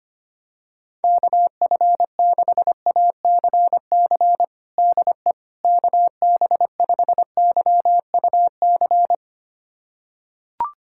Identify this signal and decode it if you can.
Morse code, heard as a keyed tone, about 25 wpm, 700 Hz. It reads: KF6ACC DE KB5YUC
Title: USA Callsign DE Callsign 25wpm